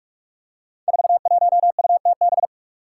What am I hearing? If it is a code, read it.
Morse code, heard as a keyed tone, about 43 wpm, 700 Hz. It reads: 41UTB
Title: Five Challenging Letter-Number Combinations 45wpm